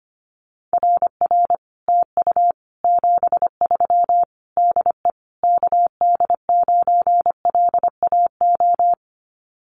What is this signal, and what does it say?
Morse code, heard as a keyed tone, about 25 wpm, 700 Hz. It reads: RR TU 73 DE KD9LAO
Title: Common QSO Phrases Encore 25wpm